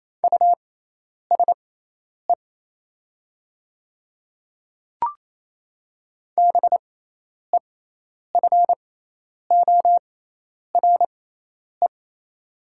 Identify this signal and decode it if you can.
Morse code, heard as a keyed tone, about 28 wpm, spaced as for 10 wpm, 690 Hz. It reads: USE BEFORE
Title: 6x Character Spacing - Top 200 Words - 28wpm